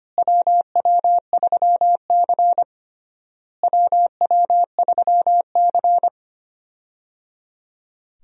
Morse code, heard as a keyed tone, about 25 wpm, 700 Hz. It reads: WW3C WW3C
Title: USA Call Signs - 4 Characters - Repeated 3x - 25wpm